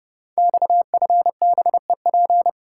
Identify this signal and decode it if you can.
Morse code, heard as a keyed tone, about 31 wpm, 700 Hz. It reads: XFBEP
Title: Code Groups - 5 Letter:Numbers - 30wpm